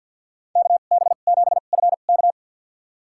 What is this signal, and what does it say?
Morse code, heard as a keyed tone, about 53 wpm, spaced as for 37 wpm, 700 Hz. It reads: KB7FK
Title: USA Call Signs - 5 Characters - 2x Character Spacing - 50wpm